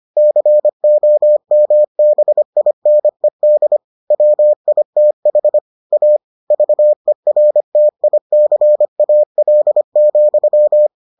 Morse code, heard as a keyed tone, about 25 wpm, 600 Hz. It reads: COMBINED WITH A VERTICAL,